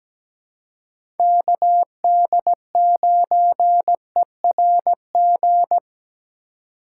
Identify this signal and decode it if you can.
Morse code, heard as a keyed tone, about 17 wpm, 700 Hz. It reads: KD9ERG
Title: USA Call Signs - 6 Characters - 4x Character Spacing - 17wpm